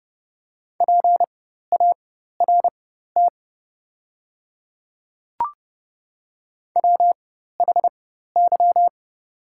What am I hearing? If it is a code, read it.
Morse code, heard as a keyed tone, about 30 wpm, spaced as for 14 wpm, 700 Hz. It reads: PART WHY
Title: 4x Character Spacing - Top 200 Words - 30wpm